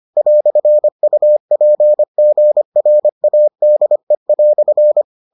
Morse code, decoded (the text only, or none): "UPGRADE"